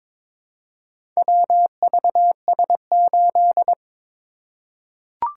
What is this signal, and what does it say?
Morse code, heard as a keyed tone, about 23 wpm, 700 Hz. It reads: WVS8
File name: practice / 22wpm / Four Letter-Number Combinations - 22wpm